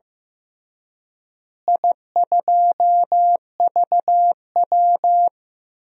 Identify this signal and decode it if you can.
Morse code, heard as a keyed tone, about 15 wpm, 700 Hz. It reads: I2VW
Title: DXCC-100 Call Signs - 4 Characters - Repeated 3x - 15wpm